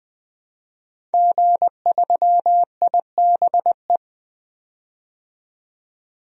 Morse code, heard as a keyed tone, about 20 wpm, 700 Hz. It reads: G3IBE
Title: DXCC-100 Call Signs - 5 Characters - 20wpm